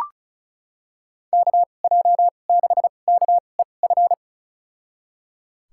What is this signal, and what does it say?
Morse code, heard as a keyed tone, about 36 wpm, spaced as for 25 wpm, 700 Hz. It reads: KJ6KEF
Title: USA Call Signs - 6 Characters - 2x Character Spacing - 35wpm